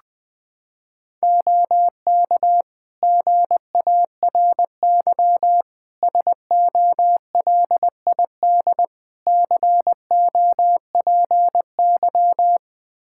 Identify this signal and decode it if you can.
Morse code, heard as a keyed tone, about 21 wpm, 700 Hz. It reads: OK GARY SOLID COPY
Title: Practice QSO - Exchange 3 - More Info - 20wpm